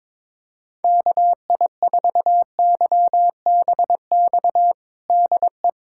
Morse code, heard as a keyed tone, about 22 wpm, 700 Hz. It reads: KI4YBX DE
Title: USA Callsign DE Callsign - 6 Characters - 22wpm